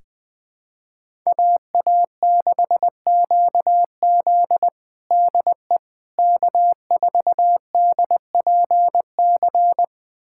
Morse code, heard as a keyed tone, about 20 wpm, 700 Hz. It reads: AA6QZ DE K4DPC
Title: USA Callsign DE Callsign - 5 Characters - 20wpm